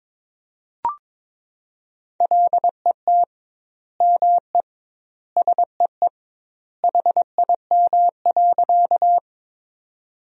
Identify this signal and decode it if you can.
Morse code, heard as a keyed tone, about 22 wpm, 700 Hz. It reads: LET ME SEE HIM.
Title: 2x Word-Spacing - Sentences from Top 300 Words - 22wpm